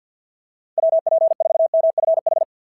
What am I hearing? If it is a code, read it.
Morse code, heard as a keyed tone, about 53 wpm, 690 Hz. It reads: WP4MFH
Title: SOTA QSO Practice 50wpm